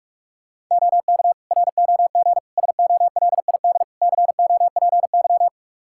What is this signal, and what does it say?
Morse code, heard as a keyed tone, about 46 wpm, 700 Hz. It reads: OK ROG SOLID COPY
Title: Practice QSO - Exchange 3 - More Info - 45wpm